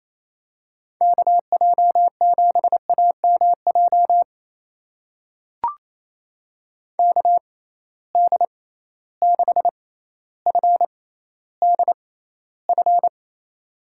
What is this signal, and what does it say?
Morse code, heard as a keyed tone, about 28 wpm, spaced as for 14 wpm, 700 Hz. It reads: KJ7AMJ KD6FDF
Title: USA Call Signs - 6 Characters - 6x Character Spacing - 28wpm